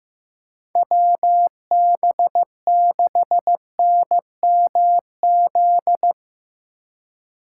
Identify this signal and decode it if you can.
Morse code, heard as a keyed tone, about 15 wpm, 700 Hz. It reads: WB6NMZ
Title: USA Call Signs - 6 Characters - 3x Character Spacing - 15wpm